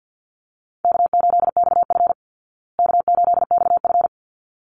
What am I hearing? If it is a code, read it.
Morse code, heard as a keyed tone, about 50 wpm, 700 Hz. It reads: K8XF K8XF
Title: Call Signs Repeated 3x - 50wpm